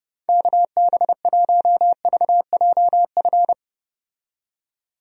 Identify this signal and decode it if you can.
Morse code, heard as a keyed tone, about 30 wpm, 700 Hz. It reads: KB1VJF
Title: USA Call Signs - 6 Characters - 30wpm